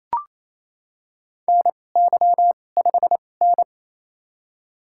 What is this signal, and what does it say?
Morse code, heard as a keyed tone, about 28 wpm, spaced as for 20 wpm, 700 Hz. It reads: NY5N
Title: USA Call Signs - 4 Characters - 2x Character Spacing - 28wpm